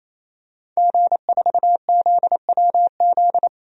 Morse code, heard as a keyed tone, about 27 wpm, 700 Hz. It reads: G4ZWZ
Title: DXCC-100 Call Signs - 5 Characters - 28wpm